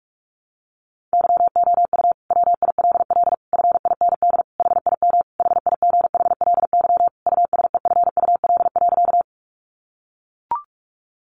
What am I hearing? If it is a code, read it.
Morse code, heard as a keyed tone, about 44 wpm, 700 Hz. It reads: YOU WILL FIND HIM HIGHLY USEFUL.
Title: Sentences from Top 1100 Words 45wpm